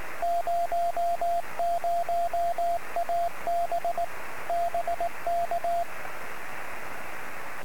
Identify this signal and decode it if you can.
Morse code, heard as a keyed tone, about 18 wpm, 690 Hz, with radio noise. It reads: 00AB BK
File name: sm7lcb / 01-oh0jfp-23cm